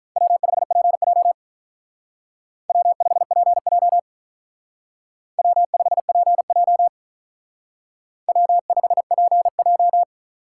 Morse code, heard as a keyed tone, about 45 wpm, 700 Hz. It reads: W5PJ W5PJ W5PJ W5PJ